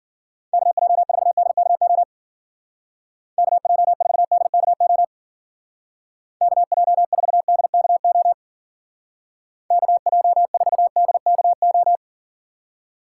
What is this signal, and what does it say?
Morse code, heard as a keyed tone, about 48 wpm, 700 Hz. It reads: KJ4DKO KJ4DKO KJ4DKO KJ4DKO
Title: Speed Racing - USA 2x3 Call Signs 40wpm